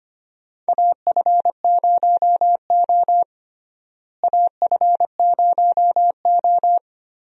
Morse code, heard as a keyed tone, about 25 wpm, 700 Hz. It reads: AF0O AF0O
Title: USA Call Signs - 4 Characters - Repeated 3x - 25wpm